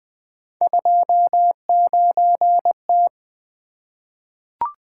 Morse code, heard as a keyed tone, about 20 wpm, 700 Hz. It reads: 29T
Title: Three Letter-Number Combinations - 20wpm